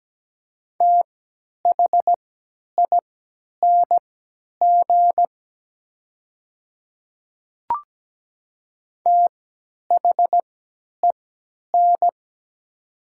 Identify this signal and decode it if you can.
Morse code, heard as a keyed tone, about 17 wpm, spaced as for 10 wpm, 700 Hz. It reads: THING THEN
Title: 3x Character Spacing - Top 300 Words - 17wpm